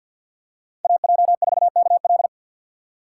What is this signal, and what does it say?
Morse code, heard as a keyed tone, about 53 wpm, 700 Hz. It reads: AJ4KL
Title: USA Call Signs - 5 Characters - 8x Character Spacing - 50wpm